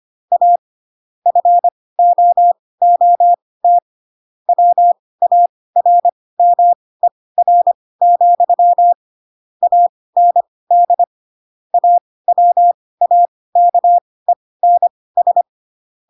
Morse code, heard as a keyed tone, about 25 wpm, spaced as for 18 wpm, 700 Hz. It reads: A FOOT WARMER, AND AWAKENS
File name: Pinocchio / Pinocchio ch6 12 25 ik1hge